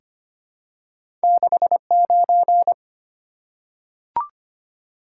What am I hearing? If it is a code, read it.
Morse code, heard as a keyed tone, about 25 wpm, 700 Hz. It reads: 69
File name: Two Letter-Number Combinations - 25wpm